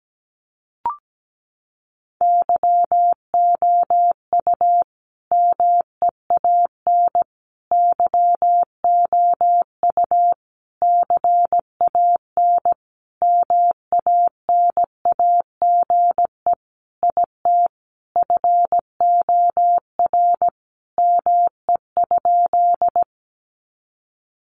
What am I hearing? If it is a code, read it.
Morse code, heard as a keyed tone, about 17 wpm, 700 Hz. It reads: YOU MEAN YOU CAN MANAGE IT FOR ME?
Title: Sentences from Top 1400 Words 17wpm